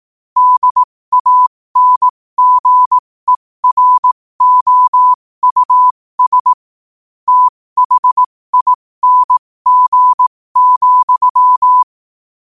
Morse code, heard as a keyed tone, about 18 wpm, spaced as for 15 wpm, 1000 Hz. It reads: DANGEROUS THING,
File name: qotdfiles / 15wpm